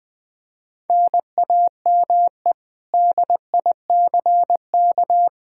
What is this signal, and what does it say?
Morse code, heard as a keyed tone, about 20 wpm, 700 Hz. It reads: NAME DICK
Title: Practice QSO - Exchange 1 - Beginning - 20wpm